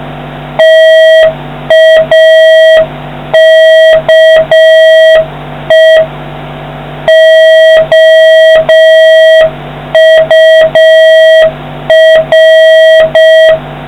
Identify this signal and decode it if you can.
Morse code, heard as a keyed tone, about 6 wpm, 1950 Hz, with noise. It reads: TAKE OUR